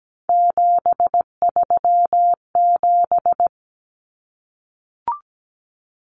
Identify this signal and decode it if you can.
Morse code, heard as a keyed tone, about 17 wpm, 690 Hz. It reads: 737
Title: Common Cut Numbers - 3-Digit 17wpm